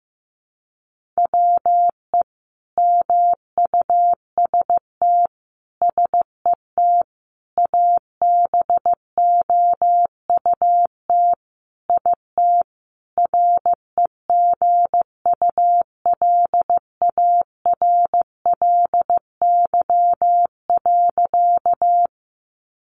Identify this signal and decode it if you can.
Morse code, heard as a keyed tone, about 15 wpm, 700 Hz. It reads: WE MUST SET ABOUT IT REGULARLY.
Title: Sentences from Top 1400 Words 15wpm